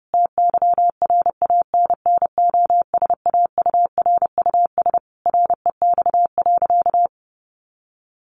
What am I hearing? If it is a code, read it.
Morse code, heard as a keyed tone, about 30 wpm, 700 Hz. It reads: TYRANNOSAURUS REX.
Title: A Sound Of Thunder 30wpm